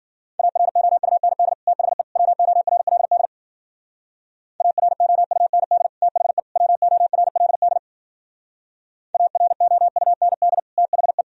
Morse code, heard as a keyed tone, about 50 wpm, 700 Hz. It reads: AROUND THE WORLD AROUND THE WORLD AROUND THE
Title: Speed-Racing - Sets of 3 Words 40wpm